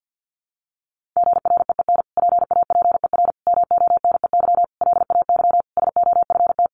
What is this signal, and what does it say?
Morse code, heard as a keyed tone, about 50 wpm, 700 Hz. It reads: GREEN PAPER MONEY LAY SOFT
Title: Sentences from Top 1500 Words 50wpm